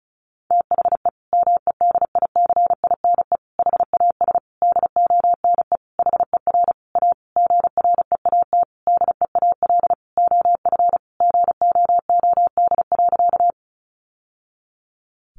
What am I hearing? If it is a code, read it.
Morse code, heard as a keyed tone, about 34 wpm, 700 Hz. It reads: THE MEDICINE HAS DONE HER A GREAT DEAL OF GOOD.
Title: Sentences from Top 900 Words 35wpm